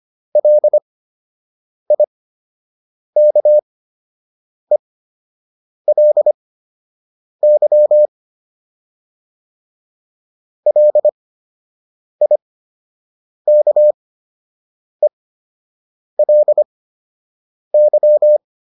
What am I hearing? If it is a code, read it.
Morse code, heard as a keyed tone, about 25 wpm, spaced as for 7 wpm, 600 Hz. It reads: LIKELY LIKELY